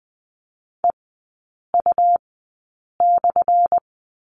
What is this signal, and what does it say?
Morse code, heard as a keyed tone, about 20 wpm, spaced as for 8 wpm, 700 Hz. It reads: EU/
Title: 178 Lesson 29e - Slash - Three Characters 20wpm